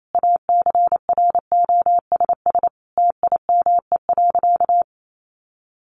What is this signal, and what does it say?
Morse code, heard as a keyed tone, about 28 wpm, 700 Hz. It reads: ACROSS TIME.
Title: A Sound Of Thunder 28wpm